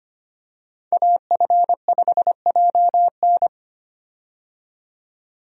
Morse code, heard as a keyed tone, about 25 wpm, 700 Hz. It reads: AF5JN
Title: USA Call Signs - 5 Characters - 25wpm